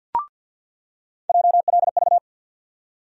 Morse code, heard as a keyed tone, about 48 wpm, 700 Hz. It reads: JLV
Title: USA Call Sign Postfixes - 3 Letters 50wpm